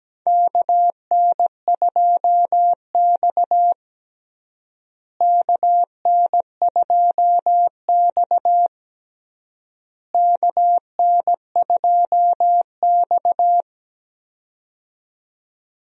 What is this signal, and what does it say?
Morse code, heard as a keyed tone, about 17 wpm, 700 Hz. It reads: KN2X KN2X KN2X
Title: USA Call Signs - 4 Characters - Repeated 3x - 17wpm